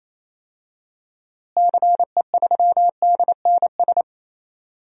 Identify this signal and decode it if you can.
Morse code, heard as a keyed tone, about 28 wpm, 700 Hz. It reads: CE3DNS
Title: DXCC-100 Call Signs - 6 Characters - Repeated 2x - 28wpm